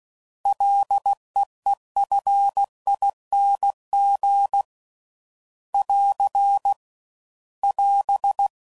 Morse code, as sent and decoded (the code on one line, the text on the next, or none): .-.. . ..-. .. -. --. / .-.-. / .-...
LEEFING <AR> <AS>